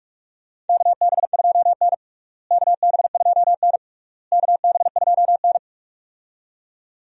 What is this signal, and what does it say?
Morse code, heard as a keyed tone, about 48 wpm, 700 Hz. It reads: KB2N KB2N KB2N
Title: USA Call Signs - 4 Characters - Repeated 3x - 45wpm